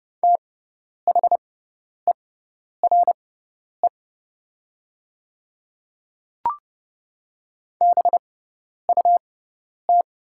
Morse code, heard as a keyed tone, about 30 wpm, spaced as for 10 wpm, 700 Hz. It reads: THERE BUT
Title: 6x Character Spacing - Top 100 Words - 30wpm